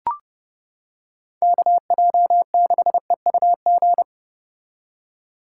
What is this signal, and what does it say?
Morse code, heard as a keyed tone, about 29 wpm, 700 Hz. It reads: KJ6EUG